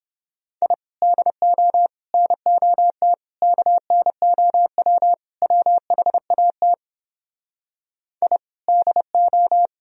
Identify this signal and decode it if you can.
Morse code, heard as a keyed tone, about 29 wpm, 700 Hz. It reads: I DO NOT KNOW WHAT I DO